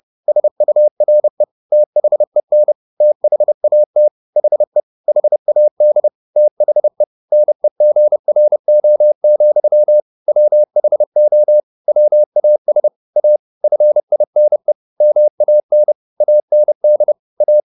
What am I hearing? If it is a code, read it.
Morse code, heard as a keyed tone, about 30 wpm, 600 Hz. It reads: SURE THEN THAT HE HAD THE NEGRO, WHO WAS A FINE MAN AND A
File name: the old man and the sea 2 30wpm